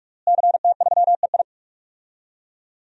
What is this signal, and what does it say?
Morse code, heard as a keyed tone, about 41 wpm, 700 Hz. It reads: CT3EI
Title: DXCC-100 Call Signs - 5 Characters - Repeated 2x - 45wpm